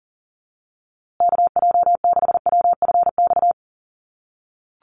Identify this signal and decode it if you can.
Morse code, heard as a keyed tone, about 40 wpm, 700 Hz. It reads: KJ6WFX
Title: USA Call Signs Repeated 3x 40wpm